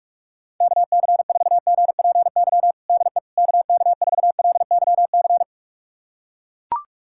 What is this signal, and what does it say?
Morse code, heard as a keyed tone, about 42 wpm, 690 Hz. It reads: KC4GPY DE KK4LYC
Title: USA Callsign DE Callsign - 6 Characters - 45wpm